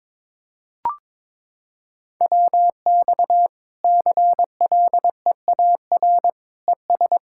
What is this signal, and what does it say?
Morse code, heard as a keyed tone, about 22 wpm, 700 Hz. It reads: WX CLEAR ES
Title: K1USN SST QSO Practice 22wpm